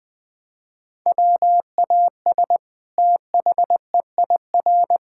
Morse code, heard as a keyed tone, about 20 wpm, 700 Hz. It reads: WAS THEIR
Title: Sentences from Top 1000 Words - Review Part 5 - 20wpm